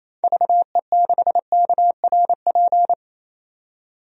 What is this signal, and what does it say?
Morse code, heard as a keyed tone, about 28 wpm, 700 Hz. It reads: VE6KRP
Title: DXCC-100 Call Signs - 6 Characters - 28wpm